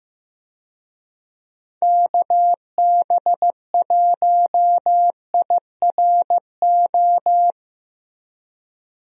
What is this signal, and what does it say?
Morse code, heard as a keyed tone, about 15 wpm, 700 Hz. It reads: KB1IRO
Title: USA Call Signs - 6 Characters - Repeated 3x - 15wpm